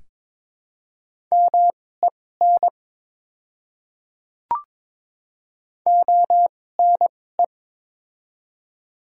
Morse code, heard as a keyed tone, about 22 wpm, spaced as for 16 wpm, 700 Hz. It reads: MEN ONE
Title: 2x Character Spacing - Top 200 Words - 22wpm